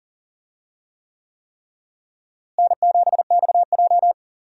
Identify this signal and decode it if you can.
Morse code, heard as a keyed tone, about 40 wpm, 700 Hz. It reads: N7XJ
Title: USA Call Signs - 4 Characters - Repeated 3x - 40wpm